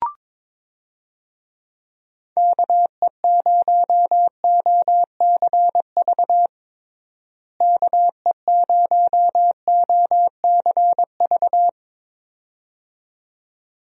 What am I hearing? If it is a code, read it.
Morse code, heard as a keyed tone, about 22 wpm, 700 Hz. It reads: KE0OCV KE0OCV
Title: USA Call Signs - 6 Characters - Repeated 2x - 22wpm